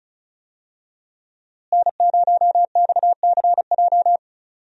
Morse code, heard as a keyed tone, about 36 wpm, 700 Hz. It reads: N0XCJ
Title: USA Call Signs - 5 Characters - Repeated 3x - 35wpm